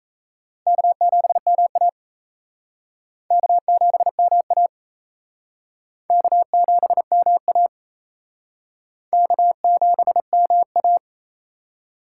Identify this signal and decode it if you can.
Morse code, heard as a keyed tone, about 35 wpm, 700 Hz. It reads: K7MA K7MA K7MA K7MA